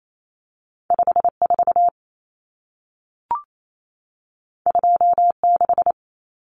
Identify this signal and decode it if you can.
Morse code, heard as a keyed tone, about 28 wpm, 700 Hz. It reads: 54 26
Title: Numbers - 2-Digit 28wpm